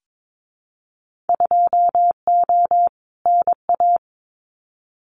Morse code, heard as a keyed tone, about 22 wpm, 700 Hz. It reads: 2O NA
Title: Winter Field Day Exchanges 22wpm